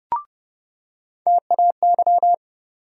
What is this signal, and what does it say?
Morse code, heard as a keyed tone, about 30 wpm, 700 Hz. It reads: TAY
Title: USA Call Sign Postfixes - 3 Letters 30wpm